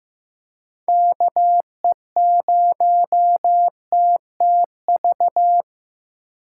Morse code, heard as a keyed tone, about 15 wpm, 700 Hz. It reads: KE0TTV
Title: USA Call Signs - 6x Character Spacing - 15wpm